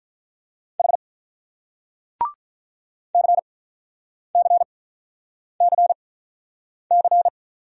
Morse code, heard as a keyed tone, about 48 wpm, spaced as for 9 wpm, 700 Hz. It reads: H CCCC